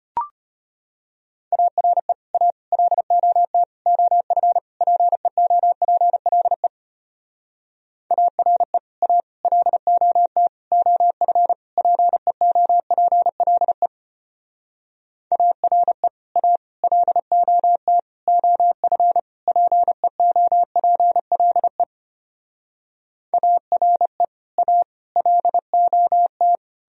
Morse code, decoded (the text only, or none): ARE A LOT OF PEOPLE ARE A LOT OF PEOPLE ARE A LOT OF PEOPLE ARE A LOT